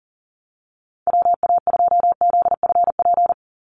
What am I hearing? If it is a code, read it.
Morse code, heard as a keyed tone, about 39 wpm, 700 Hz. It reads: WA2ZFP
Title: USA Call Signs 40wpm